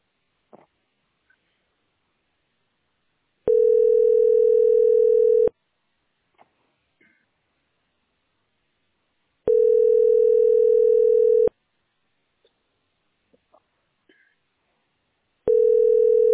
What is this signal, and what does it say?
Ringback tone (North America), 440 + 480 Hz.